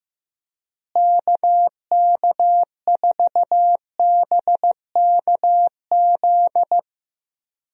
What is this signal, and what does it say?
Morse code, heard as a keyed tone, about 15 wpm, 700 Hz. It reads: KK4BKZ
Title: USA Call Signs - 6 Characters - Repeated 3x - 15wpm